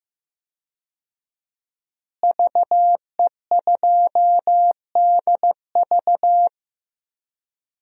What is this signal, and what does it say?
Morse code, heard as a keyed tone, about 15 wpm, 700 Hz. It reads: VE2DV
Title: DXCC-100 Call Signs - 5 Characters - Repeated 2x - 15wpm